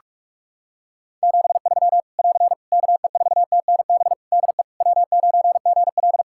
Morse code, heard as a keyed tone, about 46 wpm, 700 Hz. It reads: 73 <AR> KE4TNB DE W9GL
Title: K1USN SST QSO Practice 45wpm